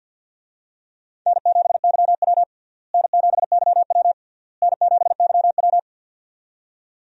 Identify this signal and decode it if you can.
Morse code, heard as a keyed tone, about 51 wpm, 700 Hz. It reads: N7YW N7YW N7YW
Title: USA Call Signs - 4 Characters - Repeated 3x - 50wpm